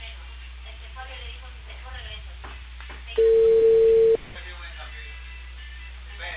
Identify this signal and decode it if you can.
Ringback tone (China, 450 Hz), with speech.